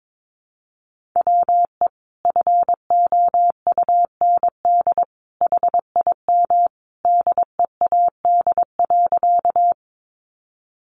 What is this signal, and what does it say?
Morse code, heard as a keyed tone, about 22 wpm, 700 Hz. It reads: WE FOUND HIM DEAD.
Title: Sentences from Top 1500 Words 22wpm